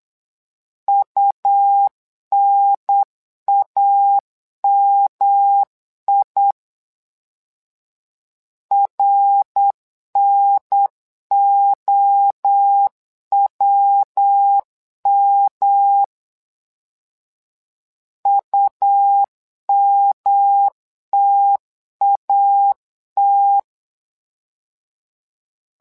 Morse code, heard as a keyed tone, about 8 wpm, 800 Hz. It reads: UNAMI RNOWM UMTAT
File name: LEZIONI / 3 MWRO / 3 MWRO 8wpm 2606